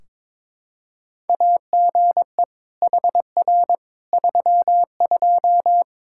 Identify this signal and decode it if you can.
Morse code, heard as a keyed tone, about 22 wpm, 700 Hz. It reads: AGE HR 32
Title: Practice QSO - Exchange 3 - More Info - 22wpm